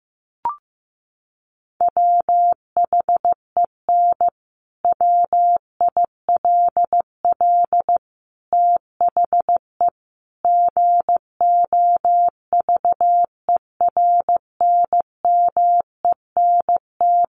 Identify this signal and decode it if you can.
Morse code, heard as a keyed tone, about 15 wpm, 700 Hz. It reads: WHEN WILL THE GOVERNMENT
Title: Sentences from Top 1400 Words 15wpm